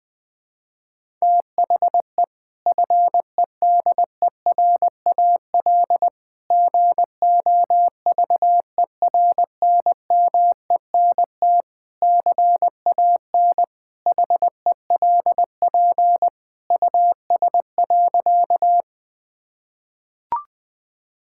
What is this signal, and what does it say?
Morse code, heard as a keyed tone, about 20 wpm, 700 Hz. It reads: THE FEDERAL GOVERNMENT CAN HELP US.
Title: Sentences from Top 1000 Words - Review Part 5 - 20wpm